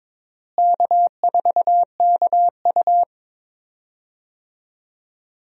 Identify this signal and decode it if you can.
Morse code, heard as a keyed tone, about 22 wpm, 700 Hz. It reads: K4KU
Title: USA Call Signs - 4 Characters - Repeated 3x - 22wpm